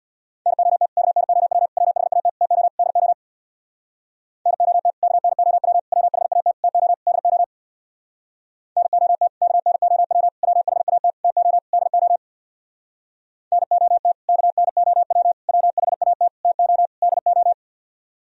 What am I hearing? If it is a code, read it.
Morse code, heard as a keyed tone, about 63 wpm, 700 Hz. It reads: NOT KNOW WHAT TO DO NOT KNOW WHAT TO DO NOT KNOW WHAT TO DO NOT KNOW WHAT TO DO